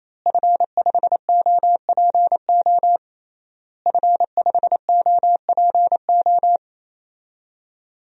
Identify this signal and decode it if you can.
Morse code, heard as a keyed tone, about 28 wpm, 700 Hz. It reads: F5OPO F5OPO
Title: DXCC-100 Call Signs - 5 Characters - Repeated 3x - 28wpm